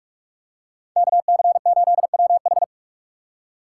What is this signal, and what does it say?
Morse code, heard as a keyed tone, about 45 wpm, 700 Hz. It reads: KC8WH
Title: K1USN SST QSO Practice - 45wpm